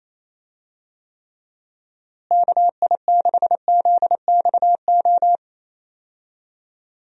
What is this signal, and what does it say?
Morse code, heard as a keyed tone, about 28 wpm, 700 Hz. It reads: KI6ZXO
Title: USA Call Signs - 6 Characters - Repeated 3x - 28wpm